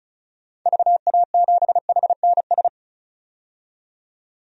Morse code, heard as a keyed tone, about 34 wpm, 700 Hz. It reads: VA7HNS
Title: DXCC-100 Call Signs - 6 Characters - 35wpm